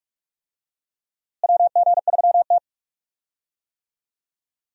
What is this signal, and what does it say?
Morse code, heard as a keyed tone, about 42 wpm, 700 Hz. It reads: WG3T
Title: USA Call Signs - 4 Characters - Repeated 2x - 45wpm